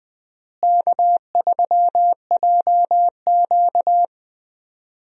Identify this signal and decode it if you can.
Morse code, heard as a keyed tone, about 20 wpm, 700 Hz. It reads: K3JQ
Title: USA Call Signs - 4 Characters - Repeated 3x - 20wpm